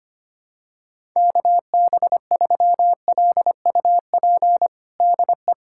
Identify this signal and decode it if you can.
Morse code, heard as a keyed tone, about 25 wpm, 700 Hz. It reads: KB3LUP DE
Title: USA Callsign DE Callsign 25wpm